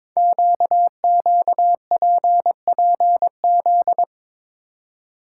Morse code, heard as a keyed tone, about 22 wpm, 700 Hz. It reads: QQPPZ
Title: Code Groups - 5 Letters - 22wpm